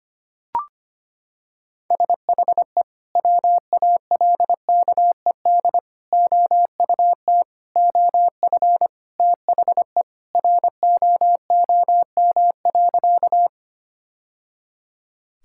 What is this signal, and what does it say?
Morse code, heard as a keyed tone, about 25 wpm, 700 Hz. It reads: SHE WALKED OUT OF THE ROOM.
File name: practice / 25wpm / Sentences with 3-Word Phrases - 25wpm